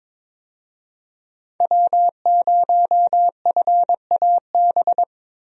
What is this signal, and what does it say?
Morse code, heard as a keyed tone, about 21 wpm, 700 Hz. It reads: W0FAB
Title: USA Call Signs - 5 Characters - 22wpm